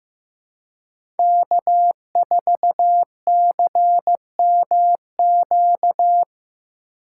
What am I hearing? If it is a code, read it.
Morse code, heard as a keyed tone, about 15 wpm, 700 Hz. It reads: K4CMQ
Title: USA Call Signs - 5 Characters - 4x Character Spacing - 15wpm